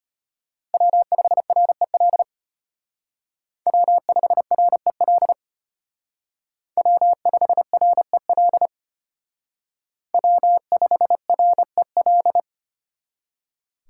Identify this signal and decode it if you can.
Morse code, heard as a keyed tone, about 31 wpm, 700 Hz. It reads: W5REL W5REL W5REL W5REL